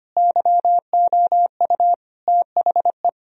Morse code, heard as a keyed tone, about 25 wpm, 700 Hz. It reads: YOU THE
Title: Sentences from Top 1000 Words - Review Part 4 - 25wpm